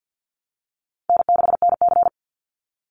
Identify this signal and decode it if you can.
Morse code, heard as a keyed tone, about 50 wpm, 700 Hz. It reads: N6NC
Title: Easy Call Signs - 50wpm